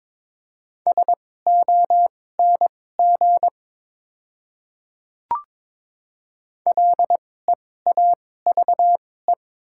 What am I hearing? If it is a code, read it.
Morse code, heard as a keyed tone, about 22 wpm, spaced as for 16 wpm, 700 Hz. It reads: SONG LEAVE
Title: 2x Character Spacing - Top 500 Words - 22wpm